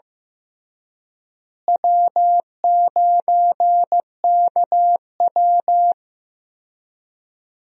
Morse code, heard as a keyed tone, about 15 wpm, 700 Hz. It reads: W9KW
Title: USA Call Signs - 4 Characters - Repeated 3x - 15wpm